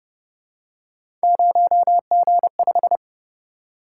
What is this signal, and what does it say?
Morse code, heard as a keyed tone, about 29 wpm, 700 Hz. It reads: 0G5